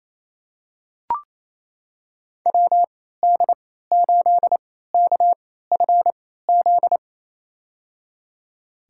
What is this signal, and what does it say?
Morse code, heard as a keyed tone, about 28 wpm, spaced as for 16 wpm, 700 Hz. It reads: WD8KFZ
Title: USA Call Signs - 3x Character Spacing - 28wpm